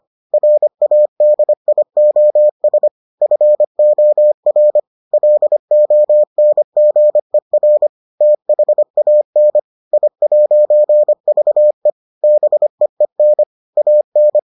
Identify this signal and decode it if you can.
Morse code, heard as a keyed tone, about 25 wpm, 600 Hz. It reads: RADIOS FOR LONGER THAN I'VE BEEN AN